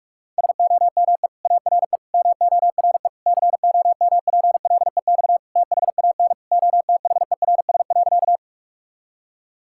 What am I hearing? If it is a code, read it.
Morse code, heard as a keyed tone, about 46 wpm, 700 Hz. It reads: SOME ARE MORE COMPLEX THAN OTHERS.
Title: Sentences from Top 1000 Words - Review Part 6 - 45wpm